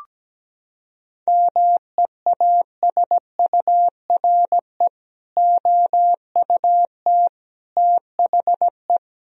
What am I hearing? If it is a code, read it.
Morse code, heard as a keyed tone, about 17 wpm, 700 Hz. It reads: MEASURE OUT THE
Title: Sentences from Top 1000 Words - Review Part 2 - 17wpm